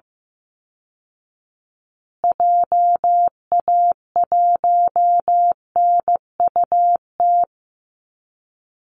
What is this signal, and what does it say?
Morse code, heard as a keyed tone, about 15 wpm, 700 Hz. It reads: JA1NUT
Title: Call Signs Repeated 2x - 15wpm